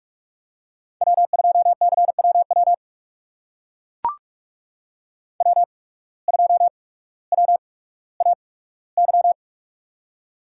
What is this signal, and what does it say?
Morse code, heard as a keyed tone, about 44 wpm, spaced as for 19 wpm, 700 Hz. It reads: W2CWW W2WAY
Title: USA Call Signs - 5 Characters - 8x Character Spacing - 45wpm